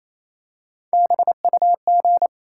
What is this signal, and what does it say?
Morse code, heard as a keyed tone, about 29 wpm, 700 Hz. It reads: BUG
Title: Three Letter Words - 28wpm